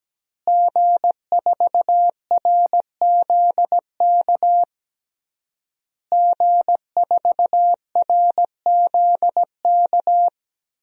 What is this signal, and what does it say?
Morse code, heard as a keyed tone, about 17 wpm, 700 Hz. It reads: G4RZK G4RZK
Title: DXCC-100 Call Signs - 5 Characters - Repeated 3x - 17wpm